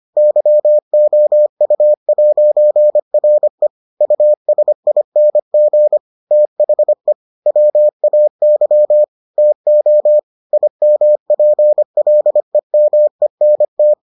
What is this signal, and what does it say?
Morse code, heard as a keyed tone, about 25 wpm, 600 Hz. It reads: YOU'RE USING THE WAY TO IMPLEMENT